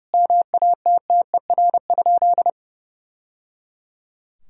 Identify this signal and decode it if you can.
Morse code, heard as a keyed tone, about 30 wpm, 700 Hz. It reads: MATTER?
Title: Sentences from Top 1000 Words - Review Part 4 - 30wpm